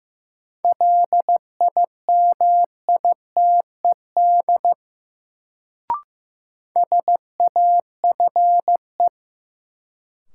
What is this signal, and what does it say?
Morse code, heard as a keyed tone, about 15 wpm, 700 Hz. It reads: LIMITED SAFE
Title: Sentences from Top 900 Words - New Word Drill - 15wpm